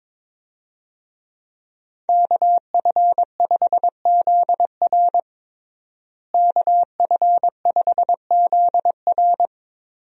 Morse code, heard as a keyed tone, about 22 wpm, 700 Hz. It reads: KF5ZR KF5ZR
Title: USA Call Signs - 5 Characters - Repeated 2x - 22wpm